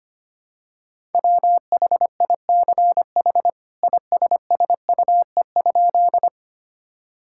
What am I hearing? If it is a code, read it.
Morse code, heard as a keyed tone, about 25 wpm, 700 Hz. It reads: WHICH ISSUE?
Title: Sentences from Top 1000 Words - Review Part 6 - 25wpm